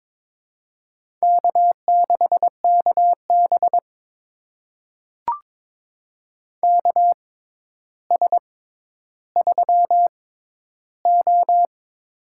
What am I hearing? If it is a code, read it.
Morse code, heard as a keyed tone, about 21 wpm, spaced as for 11 wpm, 700 Hz. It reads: K6KB KS3O
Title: USA Call Signs - 4 Characters - 6x Character Spacing - 22wpm